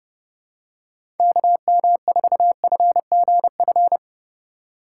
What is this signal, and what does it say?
Morse code, heard as a keyed tone, about 32 wpm, 700 Hz. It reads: KM4FGF
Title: USA Call Signs - 6 Characters - 4x Character Spacing - 30wpm